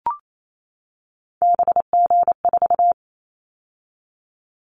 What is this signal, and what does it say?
Morse code, heard as a keyed tone, about 28 wpm, 700 Hz. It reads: BG4
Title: Callsign Prefixes for Top 100 DXCC - 28wpm